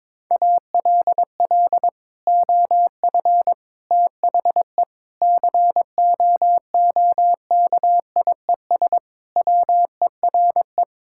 Morse code, heard as a keyed tone, about 22 wpm, 700 Hz. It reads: ALL OF THE COOKIES WERE
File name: practice / 22wpm / Sentences with 2-Word Phrases - 22wpm